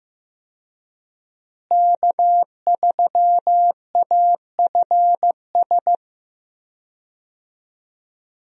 Morse code, heard as a keyed tone, about 15 wpm, 700 Hz. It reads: K3AFS
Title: USA Call Signs - 5 Characters - Repeated 2x - 15wpm